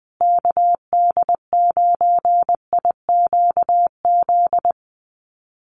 Morse code, heard as a keyed tone, about 20 wpm, 700 Hz. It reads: KD9IQZ
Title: USA Call Signs Repeated 2x 20wpm